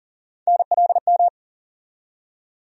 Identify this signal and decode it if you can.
Morse code, heard as a keyed tone, about 40 wpm, 700 Hz. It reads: NLM
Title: USA Call Sign Postfixes - 3 Letters 40wpm